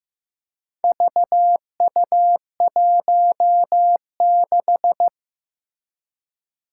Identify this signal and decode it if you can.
Morse code, heard as a keyed tone, about 15 wpm, 700 Hz. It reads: VU16
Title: Four Challenging Letter-Number Combinations 15wpm